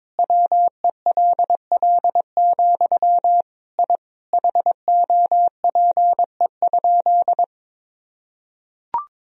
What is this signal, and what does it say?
Morse code, heard as a keyed tone, about 22 wpm, 700 Hz. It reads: WELL, I HOPE?
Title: Sentences from Top 1000 Words - Review Part 3 - 22wpm